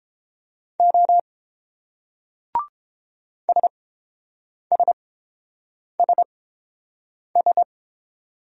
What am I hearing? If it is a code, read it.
Morse code, heard as a keyed tone, about 30 wpm, spaced as for 7 wpm, 690 Hz. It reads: O SSSS